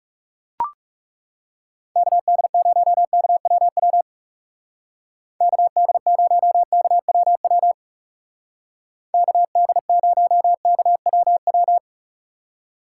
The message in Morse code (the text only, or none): KD0KWW KD0KWW KD0KWW